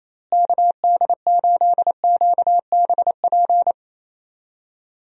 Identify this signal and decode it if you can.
Morse code, heard as a keyed tone, about 28 wpm, 700 Hz. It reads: KD8QBP
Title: USA Call Signs - 6 Characters - Repeated 3x - 28wpm